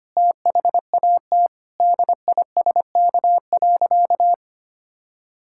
Morse code, heard as a keyed tone, about 25 wpm, 700 Hz. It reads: THAT DISK.
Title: Sentences from Top 1000 Words - Review Part 6 - 25wpm